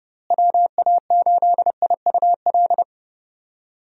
Morse code, heard as a keyed tone, about 30 wpm, 700 Hz. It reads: WA8IUL
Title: USA Call Signs - 6 Characters - Repeated 3x - 30wpm